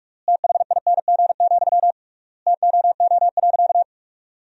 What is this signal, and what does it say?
Morse code, heard as a keyed tone, about 46 wpm, 700 Hz. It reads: THING, TOO.
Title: 3x Word-Spacing - Sentences from Top 200 Words - 45wpm